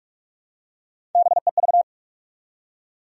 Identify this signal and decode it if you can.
Morse code, heard as a keyed tone, about 43 wpm, 700 Hz. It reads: BEV
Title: USA Common Names - Max 3 Letters - 45wpm